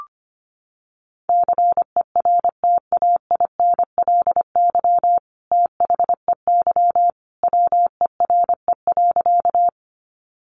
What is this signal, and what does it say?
Morse code, heard as a keyed tone, about 25 wpm, 700 Hz. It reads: CERTAINLY THEY WERE.
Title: Sentences from Top 900 Words 25wpm